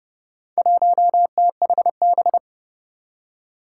Morse code, heard as a keyed tone, about 32 wpm, 700 Hz. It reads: 1THB
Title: Four Challenging Letter-Number Combinations 30wpm